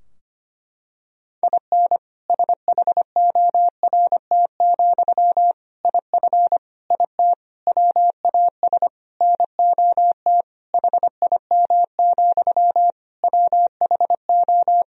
Morse code, heard as a keyed tone, about 25 wpm, 700 Hz. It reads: IN SHORT, IF IT WAS NOT HIM, WHO
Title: Sentences from Top 1000 Words - Review Part 2 - 25wpm